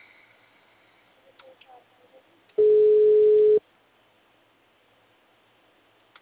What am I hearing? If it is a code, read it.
Ringback tone (Europe, 425 Hz).